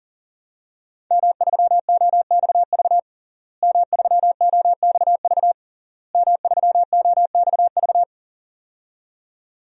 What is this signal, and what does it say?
Morse code, heard as a keyed tone, about 37 wpm, 700 Hz. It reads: M3OXV M3OXV M3OXV
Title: DXCC-100 Call Signs - 5 Characters - Repeated 3x - 40wpm